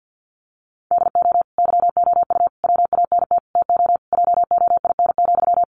Morse code, heard as a keyed tone, about 48 wpm, 700 Hz. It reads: DO YOU WANT TO JOIN,
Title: Sentences from Top 1400 Words 50wpm